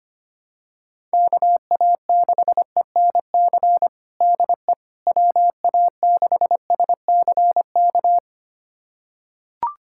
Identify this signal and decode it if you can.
Morse code, heard as a keyed tone, about 25 wpm, 700 Hz. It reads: KA6ENC DE WA6SCK
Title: USA Callsign DE Callsign - 6 Characters - 25wpm